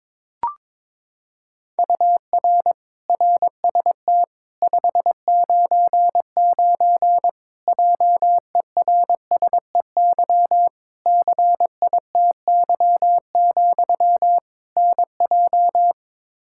Morse code, heard as a keyed tone, about 22 wpm, 700 Hz. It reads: UR RST 599 JERSEY CITY, NJ
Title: Practice QSO - Exchange 1 - Beginning - 22wpm